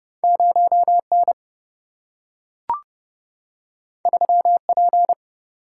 Morse code, heard as a keyed tone, about 30 wpm, 700 Hz. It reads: 0N 3P
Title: Single-Letter Single-Number 30wpm